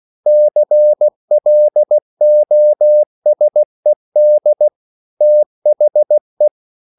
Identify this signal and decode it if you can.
Morse code, heard as a keyed tone, about 16 wpm, 600 Hz. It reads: CLOSED THE